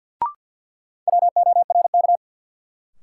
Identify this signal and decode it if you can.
Morse code, heard as a keyed tone, about 53 wpm, 700 Hz. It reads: WORK
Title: Sets of 2 Words - New Word Drill - 50wpm